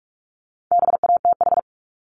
Morse code, heard as a keyed tone, about 46 wpm, 700 Hz. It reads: BATH
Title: Multi-letter Phonemes 45wpm